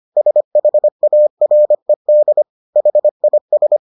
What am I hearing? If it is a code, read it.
Morse code, heard as a keyed tone, about 25 wpm, 600 Hz. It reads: SHARED HIS